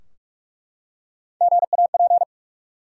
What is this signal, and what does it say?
Morse code, heard as a keyed tone, about 42 wpm, 700 Hz. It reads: GAP
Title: Three Letter Words - 45wpm